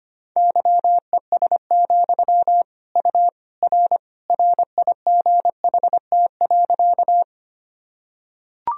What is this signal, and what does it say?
Morse code, heard as a keyed tone, about 25 wpm, 700 Hz. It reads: YES, U R RIGHT.
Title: Sentences from Top 500 Words and Abbreviations 25wpm